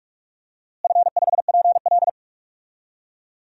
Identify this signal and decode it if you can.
Morse code, heard as a keyed tone, about 46 wpm, 700 Hz. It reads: F5PL
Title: DXCC-100 Call Signs - 4 Characters - Repeated 2x - 45wpm